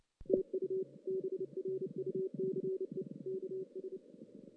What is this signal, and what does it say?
Morse code, heard as a keyed tone, about 30 wpm, 380 Hz. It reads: TU BRUCE KS
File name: 2016 JAN CW NAQP